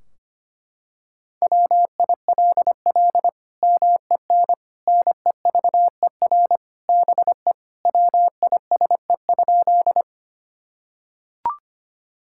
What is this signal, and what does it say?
Morse code, heard as a keyed tone, about 25 wpm, 700 Hz. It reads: WILL MEN NEVER BE WISE?
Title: Sentences from Top 1600 Words 25wpm